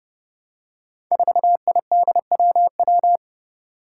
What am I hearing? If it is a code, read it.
Morse code, heard as a keyed tone, about 30 wpm, 700 Hz. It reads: 4IDWW
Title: Five Challenging Letter-Number Combinations 30wpm